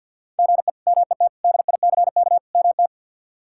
Morse code, heard as a keyed tone, about 51 wpm, 700 Hz. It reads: GE MET DICK MT
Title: K1USN SST QSO Practice - 50wpm